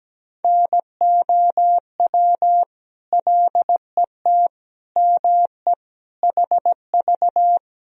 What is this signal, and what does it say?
Morse code, heard as a keyed tone, about 17 wpm, 700 Hz. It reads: NOW LET ME HV
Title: Sentences from Top 300 Words and Abbreviations 17wpm